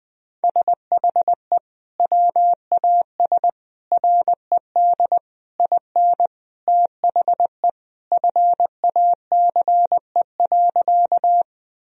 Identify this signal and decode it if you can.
Morse code, heard as a keyed tone, about 21 wpm, 700 Hz. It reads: SHE WAS RED IN THE FACE.
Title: Sentences from Top 1000 Words - Review Part 2 - 20wpm